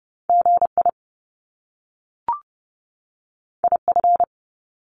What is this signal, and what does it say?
Morse code, heard as a keyed tone, about 30 wpm, 700 Hz. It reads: GI IF
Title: Top Two Letter Combinations in Words 30wpm